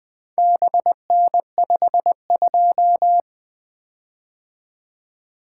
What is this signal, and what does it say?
Morse code, heard as a keyed tone, about 21 wpm, 700 Hz. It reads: BN52
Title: Four Letter-Number Combinations - 20wpm